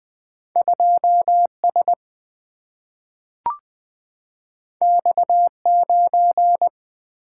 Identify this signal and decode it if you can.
Morse code, heard as a keyed tone, about 20 wpm, 700 Hz. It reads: 2S X9
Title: Single-Letter Single-Number 20wpm